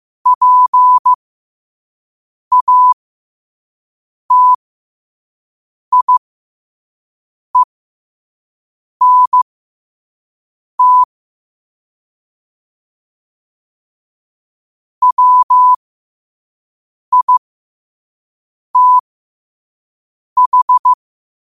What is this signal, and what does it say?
Morse code, heard as a keyed tone, about 15 wpm, spaced as for 5 wpm, 1000 Hz. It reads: PATIENT WITH